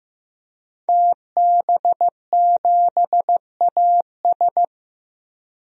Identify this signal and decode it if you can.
Morse code, heard as a keyed tone, about 15 wpm, 700 Hz. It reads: TB7AS
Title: DXCC-100 Call Signs - 5 Characters - 15wpm